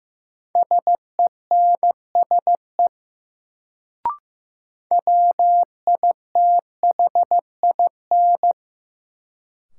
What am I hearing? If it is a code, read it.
Morse code, heard as a keyed tone, about 15 wpm, 700 Hz. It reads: SENSE WITHIN
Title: Sentences from Top 700 Words - New Word Drill - 15wpm